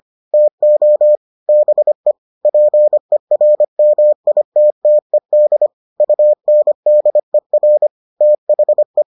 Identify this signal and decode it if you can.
Morse code, heard as a keyed tone, about 25 wpm, 600 Hz. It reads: TO BE PERMITTED UNDER THE